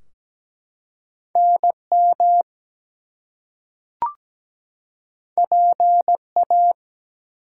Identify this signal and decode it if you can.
Morse code, heard as a keyed tone, about 17 wpm, 700 Hz. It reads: NM PA
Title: US States and Canadian Provinces 17wpm